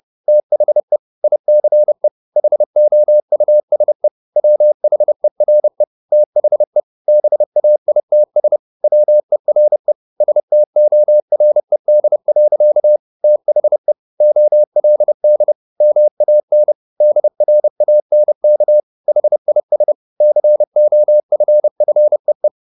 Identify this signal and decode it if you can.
Morse code, heard as a keyed tone, about 30 wpm, 600 Hz. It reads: THE ICE HOUSE WHERE THE BAITS WERE STORED. THE OLD MAN DRANK HIS COFFEE
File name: the old man and the sea / the old man and the sea 1 30wpm